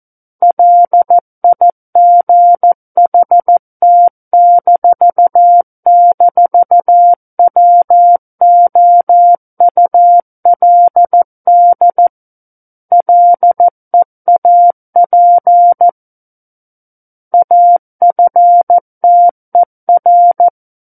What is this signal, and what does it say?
Morse code, heard as a keyed tone, about 14 wpm, 700 Hz. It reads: LIGHT--WOULD LEAP AFTER
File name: Book One / 05-Chapter 5 - 14 WPM